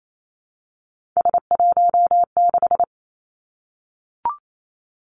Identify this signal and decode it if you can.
Morse code, heard as a keyed tone, about 28 wpm, 700 Hz. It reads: S16
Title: Three Challenging Letter-Number Combinations 28wpm